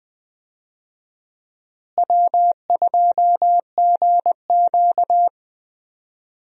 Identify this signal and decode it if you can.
Morse code, heard as a keyed tone, about 20 wpm, 700 Hz. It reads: W2GQ
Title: USA Call Signs - 4 Characters - Repeated 3x - 20wpm